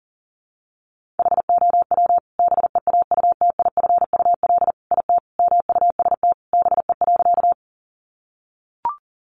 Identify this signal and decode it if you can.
Morse code, heard as a keyed tone, about 40 wpm, 700 Hz. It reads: HOW BEAUTIFUL IT MUST BE.
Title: Sentences from Top 900 Words 40wpm